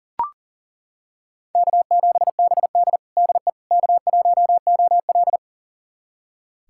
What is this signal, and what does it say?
Morse code, heard as a keyed tone, about 40 wpm, 700 Hz. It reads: K7BD DE K1OL
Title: USA Callsign DE Callsign - 4 Characters - 40wpm